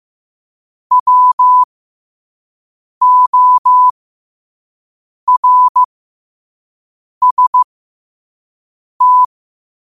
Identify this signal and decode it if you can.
Morse code, heard as a keyed tone, about 15 wpm, spaced as for 5 wpm, 1000 Hz. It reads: WORST
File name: qotdfiles / 5wpm